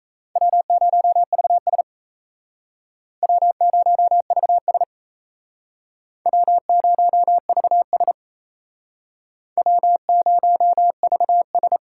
Morse code, heard as a keyed tone, about 37 wpm, 700 Hz. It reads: W0VS W0VS W0VS W0VS